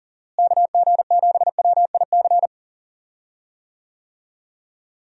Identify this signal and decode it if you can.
Morse code, heard as a keyed tone, about 37 wpm, 700 Hz. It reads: KG7WIC
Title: USA Call Signs - 6 Characters - Repeated 2x - 40wpm